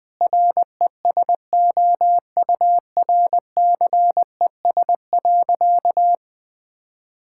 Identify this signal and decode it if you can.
Morse code, heard as a keyed tone, about 20 wpm, 700 Hz. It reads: RESOURCES.
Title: Sentences with 2-Word Phrases - 20wpm